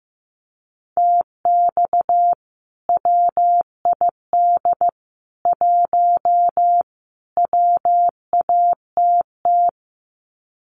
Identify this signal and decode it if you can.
Morse code, heard as a keyed tone, about 15 wpm, 700 Hz. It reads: TX WID 1 WATT
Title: Common QSO Phrases 15wpm